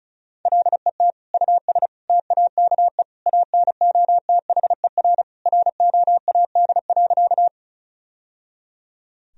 Morse code, heard as a keyed tone, about 36 wpm, 700 Hz. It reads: LET US TAKE ANOTHER ROAD.
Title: Sentences from Top 1000 Words - Review Part 2 - 35wpm